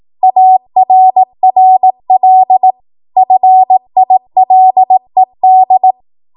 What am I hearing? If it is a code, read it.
Morse code, heard as a keyed tone, about 18 wpm, 750 Hz. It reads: ARRL FILED
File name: Morse / 18WPM